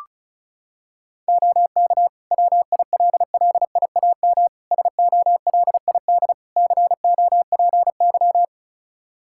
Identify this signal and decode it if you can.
Morse code, heard as a keyed tone, about 34 wpm, 700 Hz. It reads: OK WILLIAM SOLID COPY
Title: Practice QSO - Exchange 3 - More Info - 35wpm